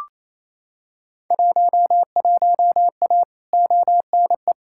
Morse code, heard as a keyed tone, about 28 wpm, 700 Hz. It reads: 11A ONE
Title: Field Day Exchanges Encore - 28wpm